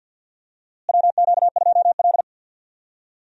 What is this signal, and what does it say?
Morse code, heard as a keyed tone, about 46 wpm, 700 Hz. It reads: WQ2L
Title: USA Call Signs - 4 Characters - Repeated 2x - 50wpm